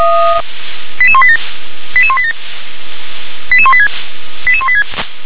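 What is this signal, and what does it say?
Five-tone selective call (ZVEI), heard as quick runs of five tones, tones of about 70 ms. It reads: 88187 88187 88186 88186
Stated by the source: ZVEI five-tone selective call received with an SDR.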